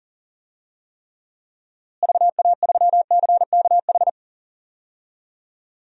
Morse code, heard as a keyed tone, about 40 wpm, 700 Hz. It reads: VA3CKH
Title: DXCC-100 Call Signs - 6 Characters - Repeated 3x - 40wpm